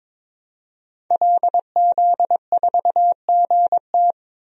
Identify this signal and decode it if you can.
Morse code, heard as a keyed tone, about 21 wpm, 700 Hz. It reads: LZ4GT
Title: DXCC-100 Call Signs - 5 Characters - 22wpm